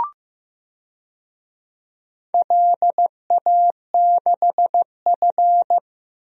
Morse code, heard as a keyed tone, about 15 wpm, 700 Hz. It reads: LA6F
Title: DXCC-100 Call Signs - 4 Characters - Repeated 3x - 15wpm